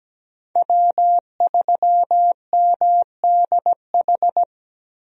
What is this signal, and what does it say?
Morse code, heard as a keyed tone, about 17 wpm, 700 Hz. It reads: W3MDH
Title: USA Call Signs - 5 Characters - Repeated 3x - 17wpm